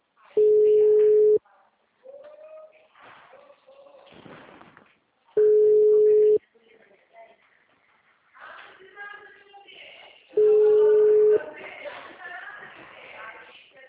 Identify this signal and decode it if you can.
Ringback tone (Europe, 425 Hz), with speech.
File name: out- -2001- - - .96497